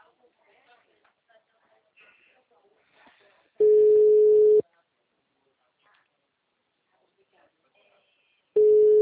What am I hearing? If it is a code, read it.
Ringback tone (Europe, 425 Hz), with speech.